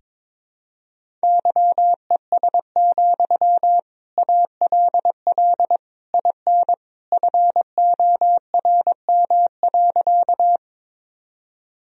Morse code, heard as a keyed tone, about 22 wpm, 700 Hz. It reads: YES, ALL IN FORM.
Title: Sentences from Top 1000 Words - Review Part 1 - 22wpm